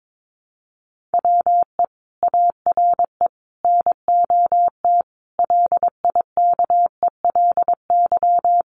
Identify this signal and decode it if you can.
Morse code, heard as a keyed tone, about 22 wpm, 700 Hz. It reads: WE ARE NOT LIKELY
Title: Sentences from Top 1100 Words 22wpm